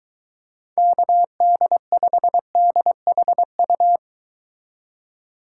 Morse code, heard as a keyed tone, about 23 wpm, 700 Hz. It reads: KD5DHU